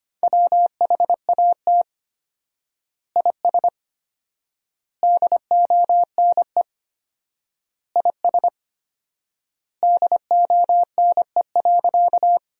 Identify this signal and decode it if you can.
Morse code, heard as a keyed tone, about 25 wpm, 700 Hz. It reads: WHAT IS DONE IS DONE.
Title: 4x Word-Spacing - Sentences from Top 500 Words - 25wpm